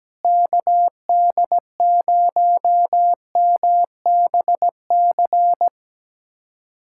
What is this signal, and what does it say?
Morse code, heard as a keyed tone, about 17 wpm, 700 Hz. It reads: KD0MBC
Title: USA Callsign DE Callsign - 6 Characters - 17wpm